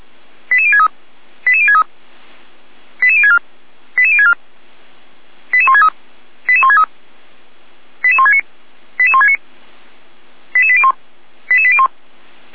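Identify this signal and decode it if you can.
Five-tone selective call (ZVEI), heard as quick runs of five tones, tones of about 70 ms. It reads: 88063 88063 88064 88064 88163 88163 88169 88169 88801 88801
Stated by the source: ZVEI five-tone call 88521 received on an SDR.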